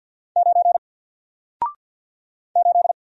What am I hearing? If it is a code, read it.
Morse code, heard as a keyed tone, about 48 wpm, 700 Hz. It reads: 9 8
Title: Single Letter-Number Alt 50wpm